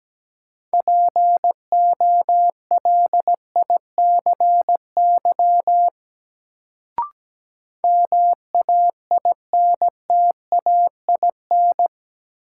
Morse code, heard as a keyed tone, about 17 wpm, 700 Hz. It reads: POLICY MAINTAIN
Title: Sentences from Top 1000 Words - Review Part 5 - New Word Drill - 17wpm